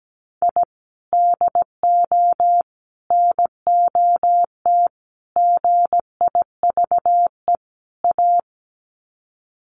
Morse code, heard as a keyed tone, about 17 wpm, 700 Hz. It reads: I DO NOT GIVE A
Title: Sets of 5 Words Encore2 17wpm